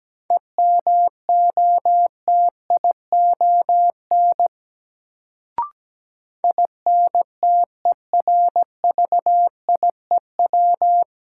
Morse code, heard as a keyed tone, about 17 wpm, 700 Hz. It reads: EMOTION INTERVIEW
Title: Sentences from Top 1600 Words - New Word Drill Alt - 17wpm